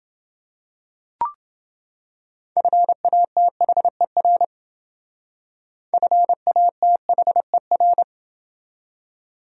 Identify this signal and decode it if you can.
Morse code, heard as a keyed tone, about 28 wpm, 700 Hz. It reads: FATHER FATHER